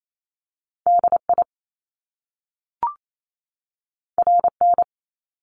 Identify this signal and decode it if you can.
Morse code, heard as a keyed tone, about 28 wpm, 700 Hz. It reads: DI RN
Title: Top Two Letter Combinations in Words 28wpm